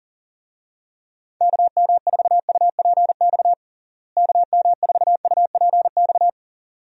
Morse code, heard as a keyed tone, about 40 wpm, 700 Hz. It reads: KM4UPX KM4UPX
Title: USA Call Signs - 6 Characters - Repeated 2x - 40wpm